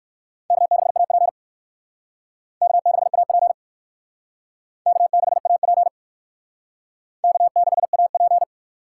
Morse code, heard as a keyed tone, about 56 wpm, 700 Hz. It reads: K6AP K6AP K6AP K6AP
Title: Speed Racing - USA 1x2 Call Signs 45wpm